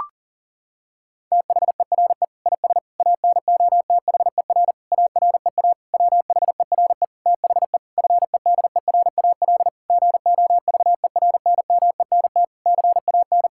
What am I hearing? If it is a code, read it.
Morse code, heard as a keyed tone, about 37 wpm, 700 Hz. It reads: THERE IS ANOTHER AREA WHERE THE FEDERAL GOVERNMENT CAN
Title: Sentences from Top 1000 Words - Review Part 5 - 40wpm